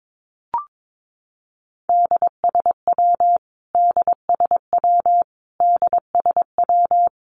Morse code, heard as a keyed tone, about 22 wpm, 700 Hz. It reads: DSW DSW DSW
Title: Abbreviations Repeated 3x - 22wpm